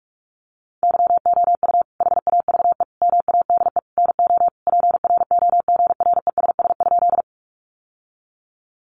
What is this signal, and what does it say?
Morse code, heard as a keyed tone, about 44 wpm, 700 Hz. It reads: YOU HAVE MADE NO PROGRESS?
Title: Sentences from Top 1400 Words 45wpm